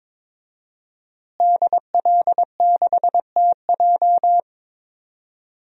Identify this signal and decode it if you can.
Morse code, heard as a keyed tone, about 22 wpm, 700 Hz. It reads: DL6TJ
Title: DXCC-100 Call Signs - 5 Characters - Repeated 3x - 22wpm